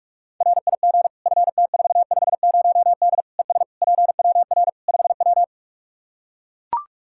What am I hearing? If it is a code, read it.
Morse code, heard as a keyed tone, about 42 wpm, 700 Hz. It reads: RIG FT450D ES PWR 5W
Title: Practice QSO - Exchange 2 - Info - 45wpm